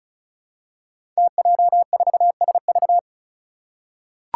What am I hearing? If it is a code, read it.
Morse code, heard as a keyed tone, about 36 wpm, 700 Hz. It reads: TJ4SV
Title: Five Challenging Letter-Number Combinations 35wpm